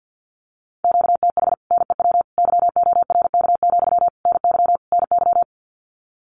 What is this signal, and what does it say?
Morse code, heard as a keyed tone, about 48 wpm, 700 Hz. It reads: QTH NEW YORK, NY NY
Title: Common QSO Phrases Encore 50wpm